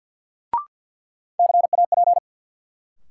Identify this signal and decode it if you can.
Morse code, heard as a keyed tone, about 51 wpm, 700 Hz. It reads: CAP
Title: Sentences from Top 1600 Words - New Word Drill - 50wpm